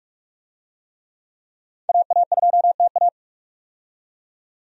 Morse code, heard as a keyed tone, about 46 wpm, 700 Hz. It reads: AA2TA
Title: USA Call Signs - 5 Characters - 45wpm